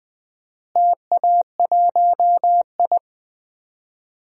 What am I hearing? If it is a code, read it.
Morse code, heard as a keyed tone, about 21 wpm, 690 Hz. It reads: TA1I
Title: DXCC-100 Call Signs - 4 Characters - Repeated 3x - 20wpm